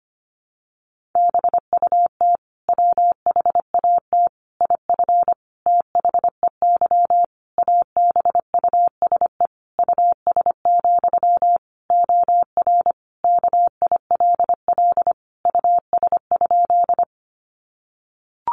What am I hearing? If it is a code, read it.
Morse code, heard as a keyed tone, about 25 wpm, 700 Hz. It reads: BUT WHAT IF THEY ABUSE US, OR KILL US?
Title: Sentences from Top 1400 Words 25wpm